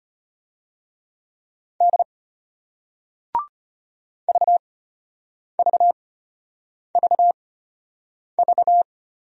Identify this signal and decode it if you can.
Morse code, heard as a keyed tone, about 34 wpm, spaced as for 8 wpm, 700 Hz. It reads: D VVVV